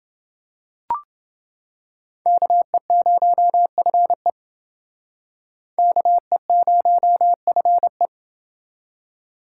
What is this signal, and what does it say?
Morse code, heard as a keyed tone, about 27 wpm, 700 Hz. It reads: KE0FE KE0FE